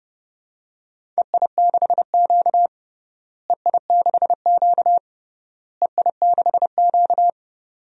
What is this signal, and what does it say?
Morse code, heard as a keyed tone, about 32 wpm, 700 Hz. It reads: EI6Q EI6Q EI6Q
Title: DXCC-100 Call Signs - 4 Characters - Repeated 3x - 30wpm